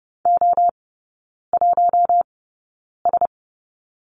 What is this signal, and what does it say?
Morse code, heard as a keyed tone, about 30 wpm, spaced as for 9 wpm, 700 Hz. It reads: O1S
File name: course / 30:10wpm / 087 Lesson 15e - Number 5 - Three Characters 30wpm